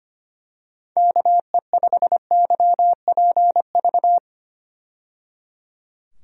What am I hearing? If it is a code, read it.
Morse code, heard as a keyed tone, about 25 wpm, 700 Hz. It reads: KE5YPV
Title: USA Call Signs - 6 Characters - 25wpm